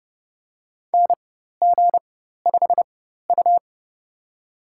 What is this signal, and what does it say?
Morse code, heard as a keyed tone, about 30 wpm, spaced as for 14 wpm, 700 Hz. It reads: NG5U
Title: USA Call Signs - 4 Characters - 4x Character Spacing - 30wpm